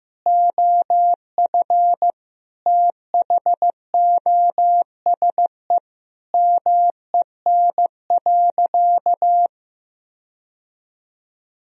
Morse code, heard as a keyed tone, about 15 wpm, 700 Hz. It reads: OF THOSE MEN.
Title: Sentences from Top 1000 Words - Review Part 1 - 15wpm